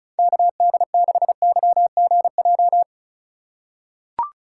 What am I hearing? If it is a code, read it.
Morse code, heard as a keyed tone, about 36 wpm, 700 Hz. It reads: KD6YGJ
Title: USA Call Signs - 6 Characters - 8x Character Spacing - 35wpm